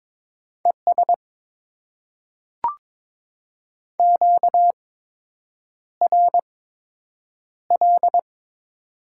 Morse code, heard as a keyed tone, about 22 wpm, spaced as for 8 wpm, 700 Hz. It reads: ES QRL
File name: Top 20 QSO Elements - 8x Character Spacing - 22wpm